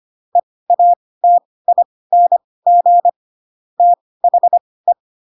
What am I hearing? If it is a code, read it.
Morse code, heard as a keyed tone, about 25 wpm, spaced as for 18 wpm, 700 Hz. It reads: EATING THE
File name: Pinocchio / Pinocchio ch7 12 25 ik1hge